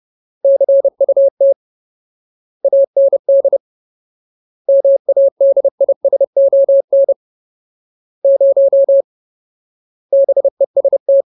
Morse code, heard as a keyed tone, about 30 wpm, 550 Hz. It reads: CUT AND MADISON 0 BEST
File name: Ultimate Word List / 30WPM Ultimate